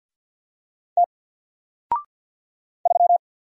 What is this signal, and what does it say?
Morse code, heard as a keyed tone, about 49 wpm, 700 Hz. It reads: T 3
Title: Single Character (Letter:Number:Punctuation:Prosign) Alt 50wpm